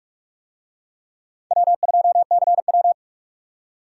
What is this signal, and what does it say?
Morse code, heard as a keyed tone, about 42 wpm, 700 Hz. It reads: W2CW